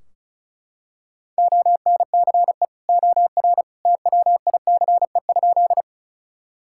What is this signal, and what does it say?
Morse code, heard as a keyed tone, about 36 wpm, 700 Hz. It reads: ONCE OR TWICE?
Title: Sentences from Top 1600 Words 35wpm